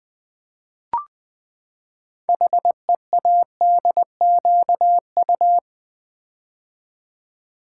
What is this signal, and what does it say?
Morse code, heard as a keyed tone, about 20 wpm, 700 Hz. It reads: HEADQU
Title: Build A Word - 6 Letters At A Time - 12-Letter Words - 20wpm